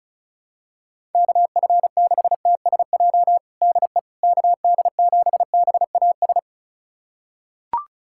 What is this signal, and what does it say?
Morse code, heard as a keyed tone, about 34 wpm, 700 Hz. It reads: KF6TSJ DE KD7BAS
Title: USA Callsign DE Callsign - 6 Characters - 35wpm